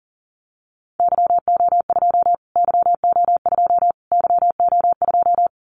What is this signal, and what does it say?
Morse code, heard as a keyed tone, about 40 wpm, 700 Hz. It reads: YO2 YO2 YO2
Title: Callsign Prefixes for Top 25 DXCC - Repeated 3x - 40wpm